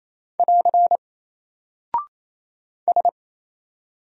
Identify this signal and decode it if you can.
Morse code, heard as a keyed tone, about 27 wpm, 700 Hz. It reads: <AR> S
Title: Single Character (Letter:Number:Punctuation:Prosign) 28wpm